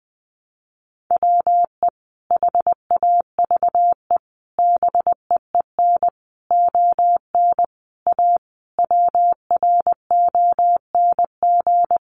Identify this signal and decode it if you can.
Morse code, heard as a keyed tone, about 20 wpm, 700 Hz. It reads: WE HAVE BEEN ON A WRONG
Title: Sentences from Top 1100 Words 20wpm